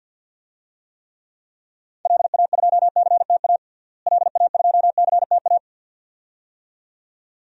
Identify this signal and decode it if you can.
Morse code, heard as a keyed tone, about 52 wpm, 700 Hz. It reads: LA2CTA LA2CTA
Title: DXCC-100 Call Signs - 6 Characters - Repeated 2x - 50wpm